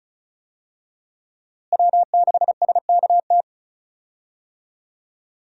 Morse code, heard as a keyed tone, about 34 wpm, 700 Hz. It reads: W6SKT
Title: USA Call Signs - 5 Characters - Repeated 2x - 35wpm